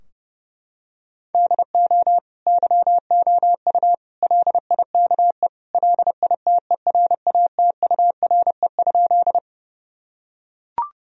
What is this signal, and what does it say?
Morse code, heard as a keyed tone, about 28 wpm, 700 Hz. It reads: DO YOU LIKE LITERATURE?
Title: Sentences from Top 1000 Words - Review Part 3 - 30wpm